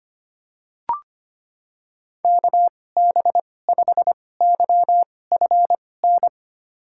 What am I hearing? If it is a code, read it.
Morse code, heard as a keyed tone, about 25 wpm, spaced as for 18 wpm, 690 Hz. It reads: KB5YFN
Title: USA Call Signs - 6 Characters - 2x Character Spacing - 25wpm